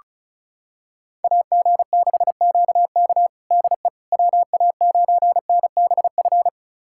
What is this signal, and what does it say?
Morse code, heard as a keyed tone, about 34 wpm, 700 Hz. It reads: AG6QK DE WA9NBF
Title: USA Callsign DE Callsign 35wpm